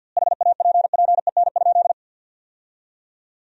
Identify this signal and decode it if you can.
Morse code, heard as a keyed tone, about 48 wpm, 700 Hz. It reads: HAPPEN?
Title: 2x Word-Spacing - Sentences from Top 400 Words - 50wpm